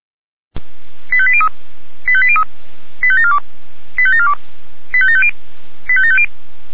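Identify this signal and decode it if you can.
Five-tone selective call (ZVEI), heard as quick runs of five tones, tones of about 70 ms. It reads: 85803 85803 85742 85742 85750 85750
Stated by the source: Five-tone ZVEI selective call received on SDR.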